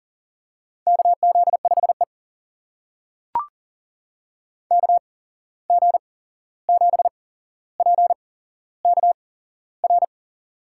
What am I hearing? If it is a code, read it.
Morse code, heard as a keyed tone, about 40 wpm, spaced as for 15 wpm, 700 Hz. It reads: KZ5E KG7PKR
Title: USA Call Signs - 8x Character Spacing - 40wpm